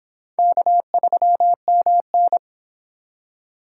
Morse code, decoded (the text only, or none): K3MN